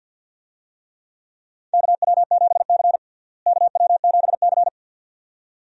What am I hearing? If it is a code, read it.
Morse code, heard as a keyed tone, about 52 wpm, 700 Hz. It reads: KW7C KW7C
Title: USA Call Signs - 4 Characters - Repeated 2x - 50wpm